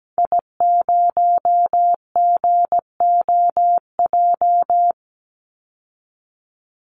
Morse code, heard as a keyed tone, about 17 wpm, 700 Hz. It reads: I0GOJ
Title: Call Signs - 17wpm